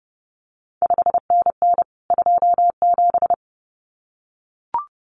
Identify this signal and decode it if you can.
Morse code, heard as a keyed tone, about 30 wpm, 700 Hz. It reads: 5NN 27
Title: CQ WW DX Contest Exchange Encore - 30wpm